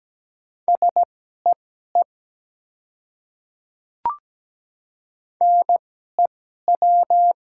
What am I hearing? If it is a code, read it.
Morse code, heard as a keyed tone, about 17 wpm, spaced as for 12 wpm, 700 Hz. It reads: SEE NEW
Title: 2x Character Spacing - Top 200 Words - 17wpm